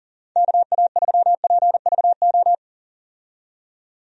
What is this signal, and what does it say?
Morse code, heard as a keyed tone, about 39 wpm, 700 Hz. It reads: KA3PVO
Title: USA Call Signs - 6 Characters - 40wpm